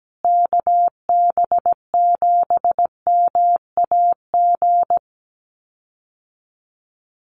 Morse code, heard as a keyed tone, about 17 wpm, 700 Hz. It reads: KB7MAG
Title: USA Call Signs Repeated 3x 17wpm